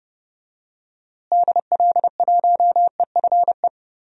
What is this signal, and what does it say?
Morse code, heard as a keyed tone, about 30 wpm, 700 Hz. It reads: DL1EFE
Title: DXCC-100 Call Signs - 6 Characters - Repeated 2x - 30wpm